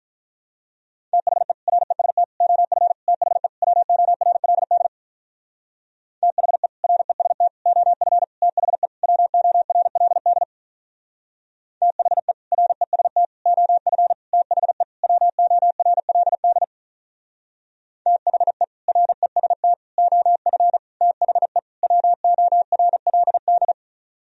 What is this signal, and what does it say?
Morse code, heard as a keyed tone, about 45 wpm, 700 Hz. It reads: THE REST OF THE WORLD THE REST OF THE WORLD THE REST OF THE WORLD THE REST OF THE WORLD